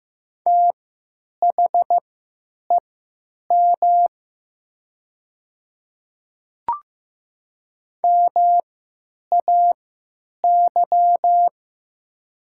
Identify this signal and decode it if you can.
Morse code, heard as a keyed tone, about 15 wpm, spaced as for 9 wpm, 690 Hz. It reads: THEM MAY
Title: 3x Character Spacing - Top 100 Words - 15wpm